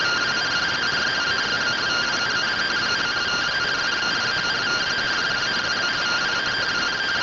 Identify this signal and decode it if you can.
Two-tone teleprinter signal, about 170 Hz shift.